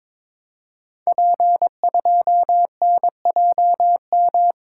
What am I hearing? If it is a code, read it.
Morse code, heard as a keyed tone, about 22 wpm, 700 Hz. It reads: P2NJM
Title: Code Groups - 5 Letter:Numbers - 22wpm